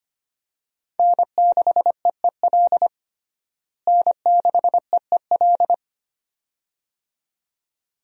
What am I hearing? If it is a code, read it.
Morse code, heard as a keyed tone, about 25 wpm, 700 Hz. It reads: N6EEL N6EEL
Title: USA Call Signs - 5 Characters - Repeated 2x - 25wpm